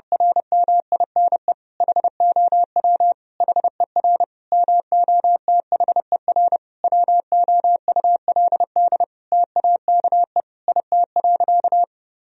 Morse code, heard as a keyed tone, about 32 wpm, 700 Hz. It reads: RMINE HOW HER MOTHER WOULD TAKE IT.
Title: Sentences from Top 1000 Words - Review Part 4 - 30wpm